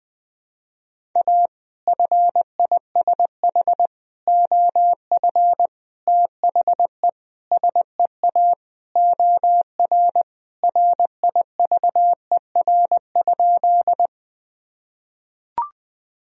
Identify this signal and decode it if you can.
Morse code, heard as a keyed tone, about 21 wpm, 700 Hz. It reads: A FISH OF THE SEA OR RIVER?
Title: Sentences from Top 1000 Words - Review Part 1 - 20wpm